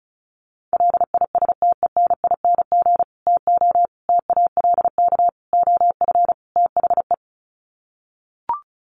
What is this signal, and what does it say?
Morse code, heard as a keyed tone, about 34 wpm, 700 Hz. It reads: LISTENING TO TALK OF THE
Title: Sets of 5 Words 35wpm